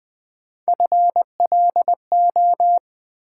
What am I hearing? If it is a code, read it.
Morse code, heard as a keyed tone, about 20 wpm, 700 Hz. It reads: FLO
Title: Build A Word - 1 Letter At A Time - 5-Letter Words - 20wpm